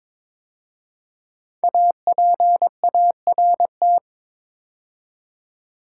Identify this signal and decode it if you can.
Morse code, heard as a keyed tone, about 22 wpm, 700 Hz. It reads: APART
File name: 5 Letter Words - First 3 Letters Given - 22wpm